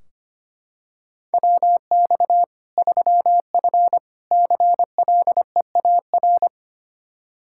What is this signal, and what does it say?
Morse code, heard as a keyed tone, about 25 wpm, 700 Hz. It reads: WX 3F CLEAR
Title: K1USN SST QSO Practice 25wpm